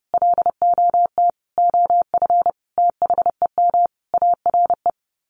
Morse code, heard as a keyed tone, about 28 wpm, 700 Hz. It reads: LOT OF THEM ARE
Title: Sets of 5 Words Encore2 30wpm